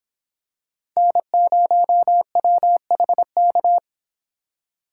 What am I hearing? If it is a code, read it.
Morse code, heard as a keyed tone, about 25 wpm, 700 Hz. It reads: N0WHK